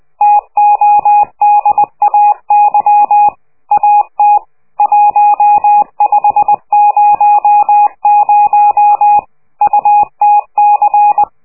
Morse code, heard as a keyed tone, about 20 wpm, 830 Hz. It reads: TODAY AT 1500 UTC
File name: worldcoastal / scheveningenradio / morsepch